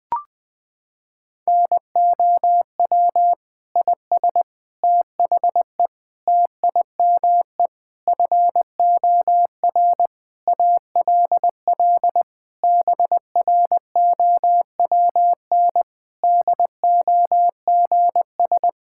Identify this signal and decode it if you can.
Morse code, heard as a keyed tone, about 20 wpm, 700 Hz. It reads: NOW IS THE TIME FOR ALL BROWN DOGS
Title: Pangram Challenge v2 - 20wpm